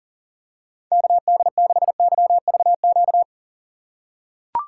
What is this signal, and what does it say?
Morse code, heard as a keyed tone, about 37 wpm, 700 Hz. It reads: KD6YVQ
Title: USA Call Signs - 6 Characters - 2x Character Spacing - 40wpm